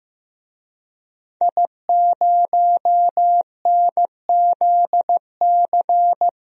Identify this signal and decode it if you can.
Morse code, heard as a keyed tone, about 15 wpm, 700 Hz. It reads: I0NZC
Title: DXCC-100 Call Signs - 5 Characters - Repeated 3x - 15wpm